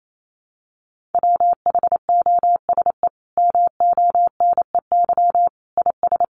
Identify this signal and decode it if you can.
Morse code, heard as a keyed tone, about 28 wpm, 700 Hz. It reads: WHOSE MONEY IS
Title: Sentences from Top 1200 Words 28wpm